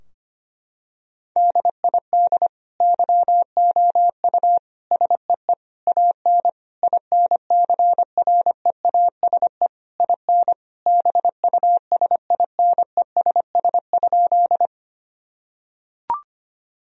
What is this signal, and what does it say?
Morse code, heard as a keyed tone, about 25 wpm, 700 Hz. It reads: DID YOU SEE AN INCREASE IN BUSINESS?
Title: Sentences from Top 1000 Words - Review Part 4 - 25wpm